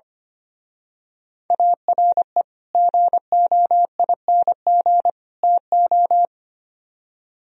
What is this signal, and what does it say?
Morse code, heard as a keyed tone, about 28 wpm, 700 Hz. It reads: ARE GOING TO
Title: Speed-Racing - Sets of 3 Words 25wpm